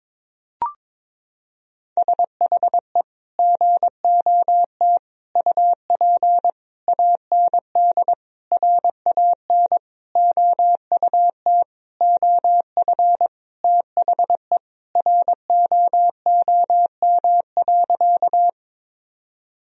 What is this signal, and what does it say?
Morse code, heard as a keyed tone, about 22 wpm, 700 Hz. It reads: SHE GOT UP AND RAN OUT OF THE ROOM.
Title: Sentences from Top 1000 Words - Review Part 3 - 22wpm